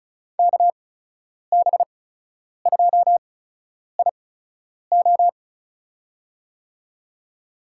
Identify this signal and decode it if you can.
Morse code, heard as a keyed tone, about 36 wpm, spaced as for 10 wpm, 700 Hz. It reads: KB2IO
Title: USA Call Signs - 5 Characters - 8x Character Spacing - 35wpm